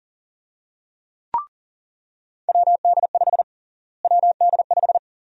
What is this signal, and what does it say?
Morse code, heard as a keyed tone, about 40 wpm, 700 Hz. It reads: WD5 WD5
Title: USA Call Sign Prefixes - Repeated 2x - 40wpm